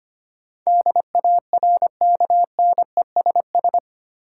Morse code, heard as a keyed tone, about 25 wpm, 700 Hz. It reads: DARKNESS
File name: 8 Letter Words - First 3 Letters Given - 25wpm